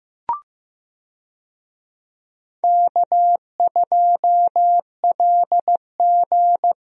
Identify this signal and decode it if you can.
Morse code, heard as a keyed tone, about 15 wpm, 700 Hz. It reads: K2LG
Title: USA Call Signs - 4 Characters - Repeated 2x - 15wpm